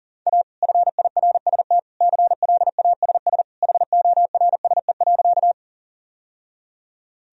Morse code, heard as a keyed tone, about 40 wpm, 700 Hz. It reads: A FIRST CLASS HORSE.
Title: Sentences from Top 1000 Words - Review Part 2 - 40wpm